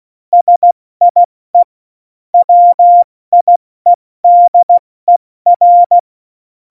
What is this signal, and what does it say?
Morse code, heard as a keyed tone, about 16 wpm, 700 Hz. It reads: SIE WIEDER
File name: Morsekurs / Witze / Witz 01. Dezember 2020, 70 F80